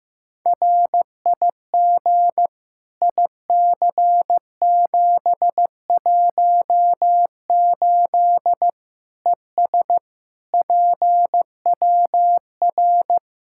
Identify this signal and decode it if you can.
Morse code, heard as a keyed tone, about 15 wpm, 700 Hz. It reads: RIG IC718 ES PWR
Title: Practice QSO - Exchange 2 - Info - 15wpm